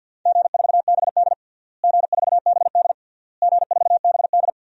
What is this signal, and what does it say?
Morse code, heard as a keyed tone, about 51 wpm, 700 Hz. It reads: G4BD G4BD G4BD
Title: DXCC-100 Call Signs - 4 Characters - Repeated 3x - 50wpm